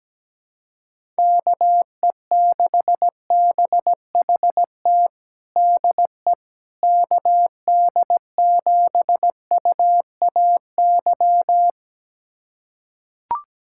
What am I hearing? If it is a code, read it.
Morse code, heard as a keyed tone, about 17 wpm, 700 Hz. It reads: KE6BHT DE KD7UAY
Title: USA Callsign DE Callsign 17wpm